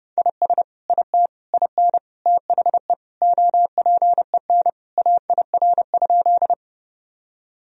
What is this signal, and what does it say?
Morse code, heard as a keyed tone, about 30 wpm, 700 Hz. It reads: IS IT IN THE OPEN AIR?
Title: Sentences from Top 1000 Words - Review Part 1 - 30wpm